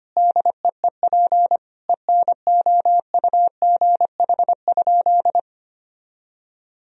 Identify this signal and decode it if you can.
Morse code, heard as a keyed tone, about 25 wpm, 700 Hz. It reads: DEEP ENOUGH?
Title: Sentences from Top 1000 Words - Review Part 3 - 25wpm